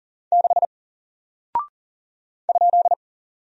Morse code, heard as a keyed tone, about 40 wpm, 700 Hz. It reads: 6 ?
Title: Single Character (Letter:Number:Punctuation:Prosign) Alt 40wpm